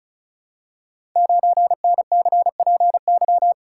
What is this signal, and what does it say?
Morse code, heard as a keyed tone, about 35 wpm, 690 Hz. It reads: 9NCPY
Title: Code Groups - 5 Letter:Numbers - 35wpm